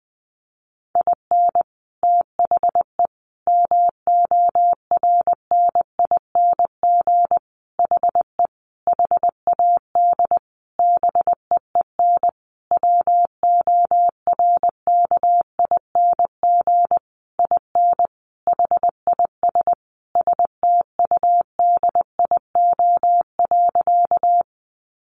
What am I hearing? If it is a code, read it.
Morse code, heard as a keyed tone, about 20 wpm, 700 Hz. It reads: IN THE MORNING HE HAD BEEN WORKING IN HIS STUDIO.
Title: Sentences from Top 1200 Words 20wpm